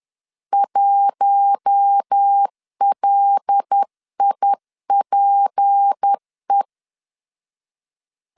Morse code, heard as a keyed tone, about 10 wpm, 800 Hz. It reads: 1LIPE
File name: LEZIONI / 6 PKX0 / 6 PKX0 10wpm 2707